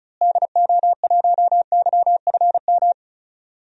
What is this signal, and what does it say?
Morse code, heard as a keyed tone, about 36 wpm, 700 Hz. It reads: DO1YFM
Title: DXCC-100 Call Signs - 6 Characters - Repeated 2x - 35wpm